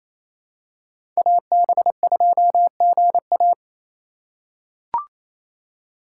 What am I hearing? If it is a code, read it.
Morse code, heard as a keyed tone, about 28 wpm, 690 Hz. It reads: AB2GA
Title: USA Call Signs - 5 Characters - 4x Character Spacing - 28wpm